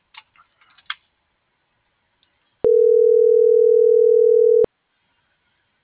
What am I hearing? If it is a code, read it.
Ringback tone (North America), 440 + 480 Hz.